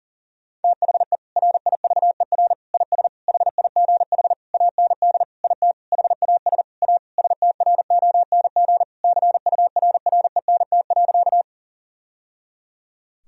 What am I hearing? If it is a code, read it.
Morse code, heard as a keyed tone, about 40 wpm, 700 Hz. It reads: THE RIVER IS HIGH AND IT HAS A STRONG CURRENT.
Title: Sentences from Top 1000 Words - Review Part 4 - 40wpm